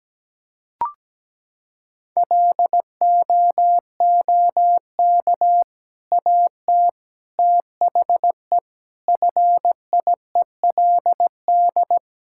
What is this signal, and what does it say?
Morse code, heard as a keyed tone, about 17 wpm, 700 Hz. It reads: LOOK AT THE FIELD
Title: Sentences from Top 1000 Words - Review Part 5 - 17wpm